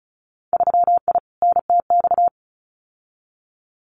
Morse code, heard as a keyed tone, about 34 wpm, 700 Hz. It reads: 3I NTX
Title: Winter Field Day Exchanges 35wpm